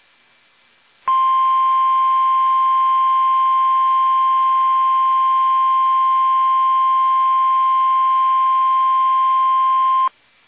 Emergency Alert System (SAME) tones (the 1050 Hz attention tone). It is heard from 1.0 to 10.1 s.
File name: eas / KUSO / KUSO